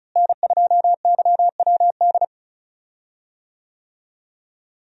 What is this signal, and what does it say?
Morse code, heard as a keyed tone, about 36 wpm, 700 Hz. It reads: N2YWD
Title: USA Call Signs - 5 Characters - Repeated 3x - 35wpm